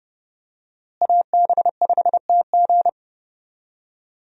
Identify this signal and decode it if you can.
Morse code, heard as a keyed tone, about 30 wpm, 700 Hz. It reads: AB5TG
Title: USA Call Signs - 5 Characters - Repeated 3x - 30wpm